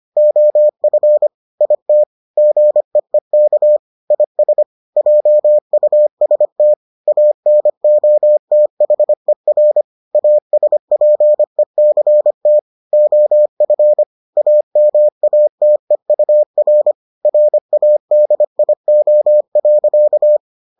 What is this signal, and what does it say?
Morse code, heard as a keyed tone, about 25 wpm, 600 Hz. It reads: OF IT GEEK IS JUST ANOTHER ASPECT OF AMATEUR RADIO.